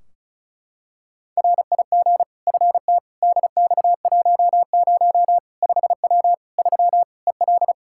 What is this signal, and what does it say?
Morse code, heard as a keyed tone, about 36 wpm, 700 Hz. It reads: RIG FT DX10 5W 3 EL
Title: Practice QSO - Exchange 2 - Info - 35wpm